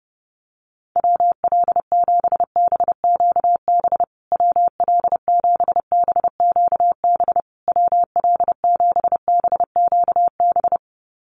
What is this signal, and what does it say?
Morse code, heard as a keyed tone, about 30 wpm, 700 Hz. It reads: WL7BQB WL7BQB WL7BQB
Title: USA Call Signs Repeated 3x 30wpm